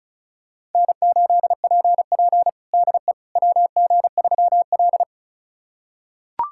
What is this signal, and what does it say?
Morse code, heard as a keyed tone, about 36 wpm, 700 Hz. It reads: N8PP DE WG3L
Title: USA Callsign DE Callsign - 4 Characters - 35wpm